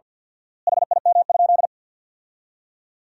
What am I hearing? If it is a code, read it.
Morse code, heard as a keyed tone, about 50 wpm, 700 Hz. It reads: HIM?
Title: 4x Word-Spacing - Sentences from Top 100 Words - 50wpm